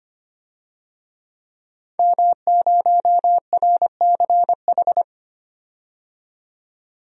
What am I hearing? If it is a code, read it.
Morse code, heard as a keyed tone, about 25 wpm, 700 Hz. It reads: M0RCH
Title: DXCC-100 Call Signs - 5 Characters - 25wpm